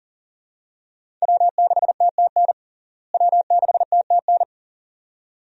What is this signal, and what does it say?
Morse code, heard as a keyed tone, about 40 wpm, 700 Hz. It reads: W6TTN W6TTN
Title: USA Call Signs - 5 Characters - Repeated 2x - 40wpm